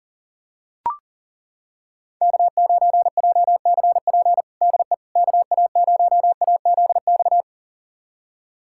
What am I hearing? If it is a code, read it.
Morse code, heard as a keyed tone, about 40 wpm, 700 Hz. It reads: K9JCP DE KA0AZX
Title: USA Callsign DE Callsign 40wpm